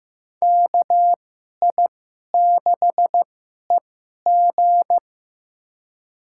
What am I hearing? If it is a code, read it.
Morse code, heard as a keyed tone, about 16 wpm, spaced as for 11 wpm, 700 Hz. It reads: KI6EG
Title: USA Call Signs - 5 Characters - 2x Character Spacing - 15wpm